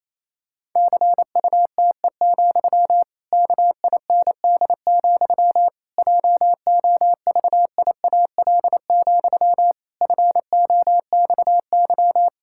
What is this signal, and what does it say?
Morse code, heard as a keyed tone, about 28 wpm, 700 Hz. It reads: CUTE, KIND, JOVIAL, FOXY
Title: Pangram Challenge v2 - 28wpm